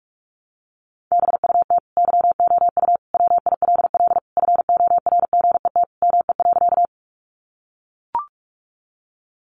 Morse code, heard as a keyed tone, about 44 wpm, 700 Hz. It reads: BUT YOU WILL FORGET ME.
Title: Sentences from Top 1200 Words 45wpm